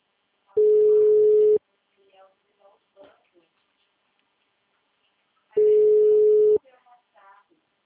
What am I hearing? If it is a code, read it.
Ringback tone (Europe, 425 Hz), with speech.